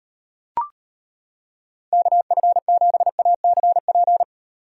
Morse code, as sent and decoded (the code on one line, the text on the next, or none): -.- ..-. --... .- -.-. .--.
KF7ACP